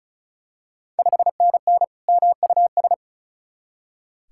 Morse code, heard as a keyed tone, about 36 wpm, 700 Hz. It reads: 5NN MUS
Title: Ontario QSO Party Exchange 35wpm